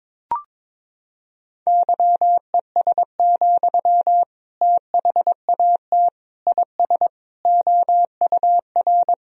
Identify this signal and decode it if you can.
Morse code, heard as a keyed tone, about 21 wpm, 700 Hz. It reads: YES, THAT IS OUR
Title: Sentences from Top 1000 Words - Review Part 3 - 22wpm